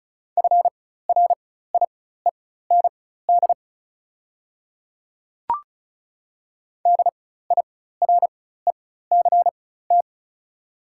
Morse code, heard as a keyed tone, about 36 wpm, spaced as for 16 wpm, 700 Hz. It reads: FRIEND DIRECT
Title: 4x Character Spacing - Top 500 Words - 35wpm